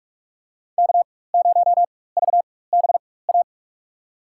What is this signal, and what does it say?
Morse code, heard as a keyed tone, about 46 wpm, spaced as for 21 wpm, 700 Hz. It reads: K0VBA
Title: USA Call Signs - 4x Character Spacing - 45wpm